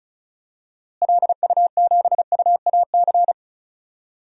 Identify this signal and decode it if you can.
Morse code, heard as a keyed tone, about 36 wpm, 700 Hz. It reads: LU7UAC
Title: DXCC-100 Call Signs - 6 Characters - Repeated 3x - 35wpm